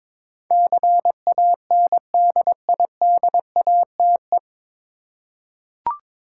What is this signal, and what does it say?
Morse code, heard as a keyed tone, about 21 wpm, 700 Hz. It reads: CANDIDATE
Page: Morse Code Ninja practice 22wpm